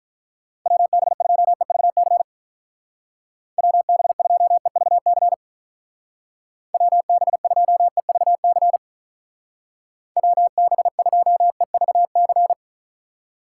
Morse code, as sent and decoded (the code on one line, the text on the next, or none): .-- -... ..--- . ...- -.-. / .-- -... ..--- . ...- -.-. / .-- -... ..--- . ...- -.-. / .-- -... ..--- . ...- -.-.
WB2EVC WB2EVC WB2EVC WB2EVC